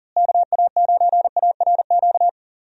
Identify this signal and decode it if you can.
Morse code, heard as a keyed tone, about 40 wpm, 700 Hz. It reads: KA9ARQ
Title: USA Call Signs - 6 Characters - Repeated 2x - 40wpm